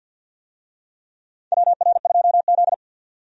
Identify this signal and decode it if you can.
Morse code, heard as a keyed tone, about 52 wpm, 700 Hz. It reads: WR2Z
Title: USA Call Signs - 4 Characters - 50wpm